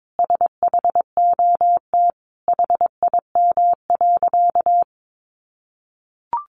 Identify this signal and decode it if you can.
Morse code, heard as a keyed tone, about 22 wpm, 700 Hz. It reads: SHOT HIM.
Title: Sentences from Top 1100 Words 22wpm